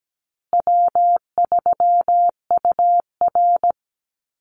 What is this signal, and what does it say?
Morse code, heard as a keyed tone, about 17 wpm, 700 Hz. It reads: W3UR
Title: Easy Call Signs Repeated 3x - 17wpm